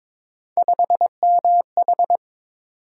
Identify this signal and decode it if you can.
Morse code, heard as a keyed tone, about 21 wpm, 700 Hz. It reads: 5MH
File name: practice / 22wpm / Three Letter-Number Combinations - 22wpm